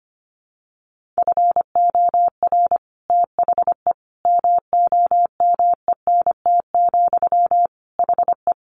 Morse code, heard as a keyed tone, about 25 wpm, 700 Hz. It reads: FOR THE MOMENT, HE
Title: Sentences from Top 1400 Words 25wpm